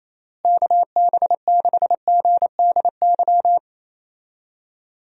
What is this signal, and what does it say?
Morse code, heard as a keyed tone, about 28 wpm, 700 Hz. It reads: KB6GDY
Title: USA Call Signs - 6 Characters - Repeated 3x - 28wpm